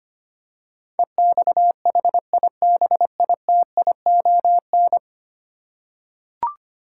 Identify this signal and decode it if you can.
Morse code, heard as a keyed tone, about 25 wpm, 700 Hz. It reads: EXHIBITION
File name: 10 Letter Words - First 3 Letters Given - 25wpm